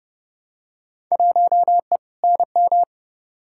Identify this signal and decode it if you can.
Morse code, heard as a keyed tone, about 30 wpm, 700 Hz. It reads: 1E NM
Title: Field Day Exchanges - 30wpm